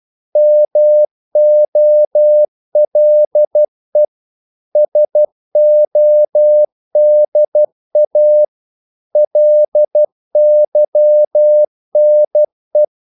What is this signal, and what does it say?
Morse code, heard as a keyed tone, about 12 wpm, 600 Hz. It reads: MOLE SODA LYNE
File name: Korte ord DK 12wpm